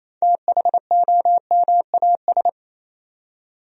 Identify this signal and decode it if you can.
Morse code, heard as a keyed tone, about 28 wpm, 700 Hz. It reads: THOMAS
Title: USA Common Names - 28wpm